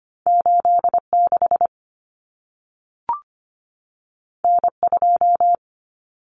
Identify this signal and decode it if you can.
Morse code, heard as a keyed tone, about 25 wpm, 700 Hz. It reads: 86 N2
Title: Common Cut Numbers - 2-Digit 25wpm